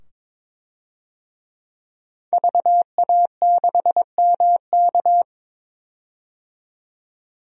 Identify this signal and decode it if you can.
Morse code, heard as a keyed tone, about 22 wpm, 700 Hz. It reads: VA6MK
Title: DXCC-100 Call Signs - 5 Characters - 22wpm